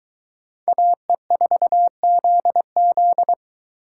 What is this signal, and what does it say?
Morse code, heard as a keyed tone, about 22 wpm, 700 Hz. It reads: AE4ZZ